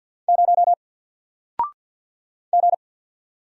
Morse code, heard as a keyed tone, about 50 wpm, 700 Hz. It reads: 0 G
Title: Single Letter-Number Alt 50wpm